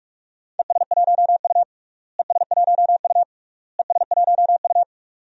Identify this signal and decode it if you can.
Morse code, heard as a keyed tone, about 44 wpm, 700 Hz. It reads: ES1U ES1U ES1U
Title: DXCC-100 Call Signs - 4 Characters - Repeated 3x - 45wpm